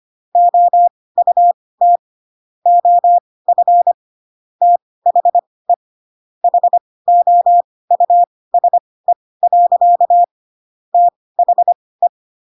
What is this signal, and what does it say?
Morse code, heard as a keyed tone, about 25 wpm, spaced as for 18 wpm, 700 Hz. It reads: OUT OF THE HOUSE. THE
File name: Pinocchio / Pinocchio ch6 12 25 ik1hge